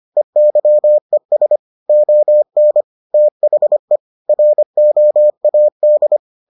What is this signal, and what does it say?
Morse code, heard as a keyed tone, about 25 wpm, 600 Hz. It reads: EYES ON THE ROAD